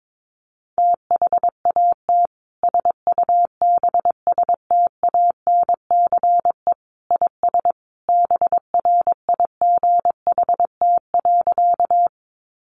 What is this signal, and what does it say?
Morse code, heard as a keyed tone, about 22 wpm, 700 Hz. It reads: THAT SUBSTANCE IS BRIGHT.
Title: Sentences from Top 1500 Words 22wpm